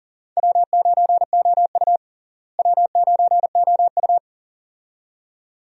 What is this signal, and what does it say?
Morse code, heard as a keyed tone, about 40 wpm, 700 Hz. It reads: W9OU W9OU
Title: USA Call Signs - 4 Characters - Repeated 2x - 40wpm